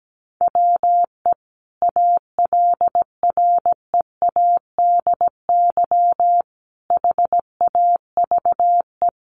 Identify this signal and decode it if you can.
Morse code, heard as a keyed tone, about 17 wpm, 700 Hz. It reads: WE ALREADY HAVE
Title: Sentences from Top 700 Words 17wpm